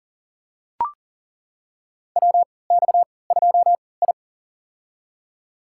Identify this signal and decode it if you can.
Morse code, heard as a keyed tone, about 41 wpm, spaced as for 23 wpm, 700 Hz. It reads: WX2I
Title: USA Call Signs - 4 Characters - 3x Character Spacing - 40wpm